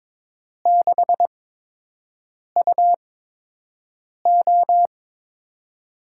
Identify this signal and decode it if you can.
Morse code, heard as a keyed tone, about 22 wpm, spaced as for 6 wpm, 700 Hz. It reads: 6UO
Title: USA Call Signs - 5 Characters - 8x Character Spacing - 22wpm